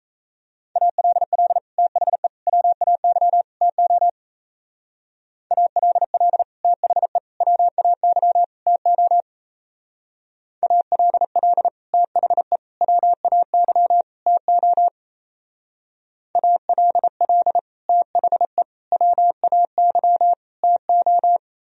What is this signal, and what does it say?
Morse code, heard as a keyed tone, about 34 wpm, 700 Hz. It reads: ALL THE WAY TO ALL THE WAY TO ALL THE WAY TO ALL THE WAY TO
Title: Speed-Racing - Sets of 4 Words 28wpm